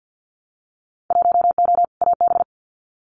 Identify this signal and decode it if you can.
Morse code, heard as a keyed tone, about 50 wpm, 700 Hz. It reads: 1O AB
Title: Winter Field Day Exchanges Encore 50wpm